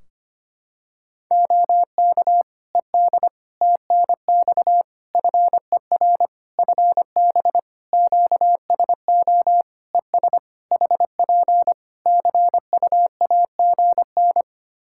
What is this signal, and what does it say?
Morse code, heard as a keyed tone, about 25 wpm, 700 Hz. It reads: OK ED TNX FER FB QSO ES HP CUAGN
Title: Practice QSO - Exchange 4 - Ending - 25wpm